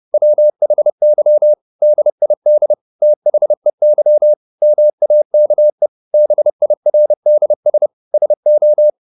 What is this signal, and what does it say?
Morse code, heard as a keyed tone, about 30 wpm, 600 Hz. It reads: WHY DID THEY MAKE BIRDS SO
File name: the old man and the sea / the old man and the sea 1 30wpm